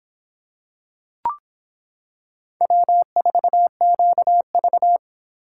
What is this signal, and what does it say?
Morse code, heard as a keyed tone, about 25 wpm, 700 Hz. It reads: W4QV